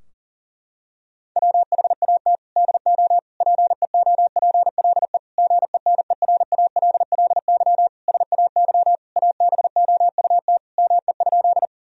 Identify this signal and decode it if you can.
Morse code, heard as a keyed tone, about 37 wpm, 700 Hz. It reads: WHAT DO PEOPLE GENERALLY SAY ABOUT ME?
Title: Sentences from Top 1000 Words - Review Part 4 - 40wpm